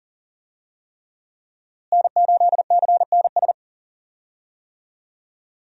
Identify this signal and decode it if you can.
Morse code, heard as a keyed tone, about 37 wpm, 700 Hz. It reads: N8CNS
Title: USA Call Signs - 5 Characters - Repeated 3x - 40wpm